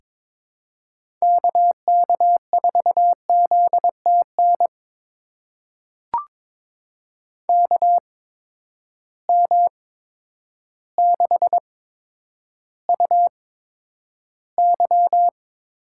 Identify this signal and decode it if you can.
Morse code, heard as a keyed tone, about 22 wpm, spaced as for 10 wpm, 700 Hz. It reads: KK4ZTN KM6UY
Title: USA Call Signs - 6 Characters - 8x Character Spacing - 22wpm